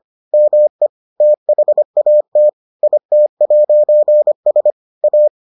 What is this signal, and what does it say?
Morse code, heard as a keyed tone, about 25 wpm, 600 Hz. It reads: ME THAT IT'S A